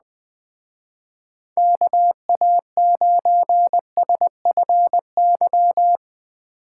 Morse code, heard as a keyed tone, about 21 wpm, 700 Hz. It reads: KA9SFY